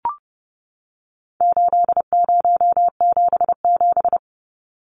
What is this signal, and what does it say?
Morse code, heard as a keyed tone, about 30 wpm, 700 Hz. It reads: 8077
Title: Numbers - 4-Digit 30wpm